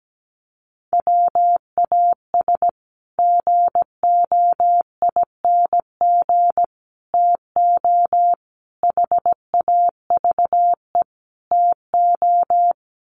Morse code, heard as a keyed tone, about 17 wpm, 700 Hz. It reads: WAS GOING TO HAVE TO
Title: Sets of 5 Words Encore 17wpm